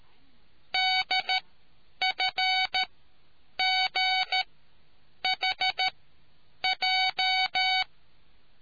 Morse code, heard as a keyed tone, about 13 wpm, spaced as for 8 wpm, 2250 Hz. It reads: DFGHJ